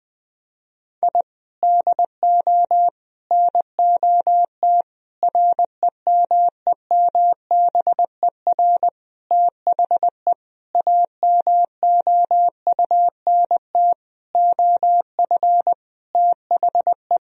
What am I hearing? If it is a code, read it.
Morse code, heard as a keyed tone, about 21 wpm, 700 Hz. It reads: I DO NOT REMEMBER THE AMOUNT OF THE
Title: Sentences from Top 1000 Words - Review Part 4 - 20wpm